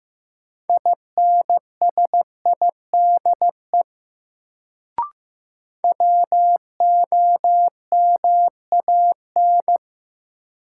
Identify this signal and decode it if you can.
Morse code, heard as a keyed tone, about 15 wpm, 700 Hz. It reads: INSIDE WOMAN
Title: Sentences from Top 1000 Words - Review Part 2 - New Word Drill - 15wpm